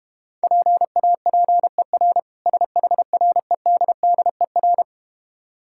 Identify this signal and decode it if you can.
Morse code, heard as a keyed tone, about 30 wpm, 700 Hz. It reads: PAPER SHREDDER
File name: Easy Words - Speed-Racing with Hints - 25wpm